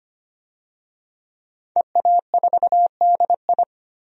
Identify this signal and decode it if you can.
Morse code, heard as a keyed tone, about 25 wpm, 700 Hz. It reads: EA4DI
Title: DXCC-100 Call Signs - 5 Characters - 25wpm